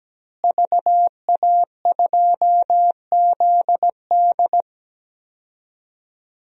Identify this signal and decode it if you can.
Morse code, heard as a keyed tone, about 17 wpm, 700 Hz. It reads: VA2ZD
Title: DXCC-100 Call Signs - 5 Characters - Repeated 3x - 17wpm